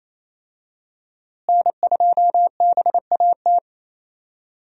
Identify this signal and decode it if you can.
Morse code, heard as a keyed tone, about 28 wpm, 700 Hz. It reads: N2BAT
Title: USA Call Signs - 5 Characters - Repeated 2x - 28wpm